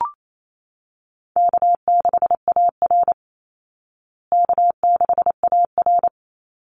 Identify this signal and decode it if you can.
Morse code, heard as a keyed tone, about 28 wpm, 700 Hz. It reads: K6AR K6AR
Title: Easy Call Signs Repeated 3x - 28wpm